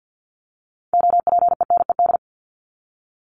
Morse code, heard as a keyed tone, about 50 wpm, 700 Hz. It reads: OPENED
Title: ED Suffix - Top 100 Words with Base Words 50wpm